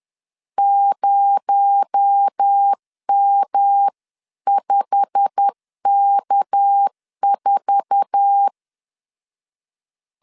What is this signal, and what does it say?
Morse code, heard as a keyed tone, about 10 wpm, 800 Hz. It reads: 0M5K4
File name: CW / DET / 1 9 1500C S A 10wpm 4511